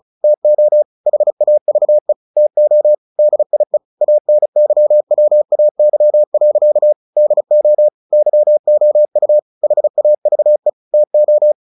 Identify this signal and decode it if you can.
Morse code, heard as a keyed tone, about 35 wpm, 600 Hz. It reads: TO HAVE TO DIE ANYWAY. DO YOU HAVE TO
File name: the old man and the sea 2 35wpm